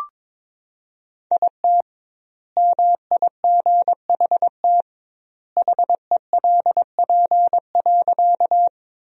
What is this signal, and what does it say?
Morse code, heard as a keyed tone, about 22 wpm, 700 Hz. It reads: IT MIGHT HELP.
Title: 2x Word-Spacing - Sentences from Top 300 Words - 22wpm